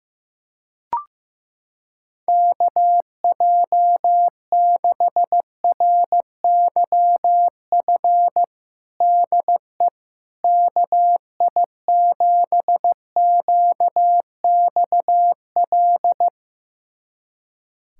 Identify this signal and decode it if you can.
Morse code, heard as a keyed tone, about 15 wpm, 700 Hz. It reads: KJ6RYF DE KI7QXL
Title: USA Callsign DE Callsign - 6 Characters - 15wpm